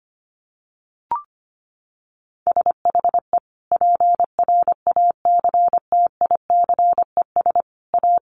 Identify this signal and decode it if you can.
Morse code, heard as a keyed tone, about 25 wpm, 700 Hz. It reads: SHE PRACTICES A
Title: Additional Easy Phrases 25wpm